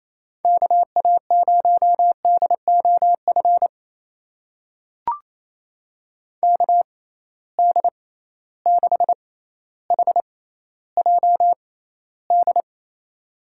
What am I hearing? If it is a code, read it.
Morse code, heard as a keyed tone, about 28 wpm, spaced as for 14 wpm, 700 Hz. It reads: KA0DOF KD6HJD
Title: USA Call Signs - 6 Characters - 6x Character Spacing - 28wpm